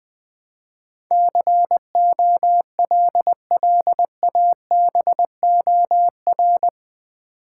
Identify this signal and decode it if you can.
Morse code, heard as a keyed tone, about 20 wpm, 700 Hz. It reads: COLLABOR
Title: Build A Word - 4 Letters At A Time - 11-Letter Words - 20wpm